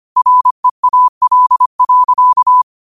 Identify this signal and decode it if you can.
Morse code, heard as a keyed tone, about 25 wpm, 1000 Hz. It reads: REAL.
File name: qotdfiles / 25wpm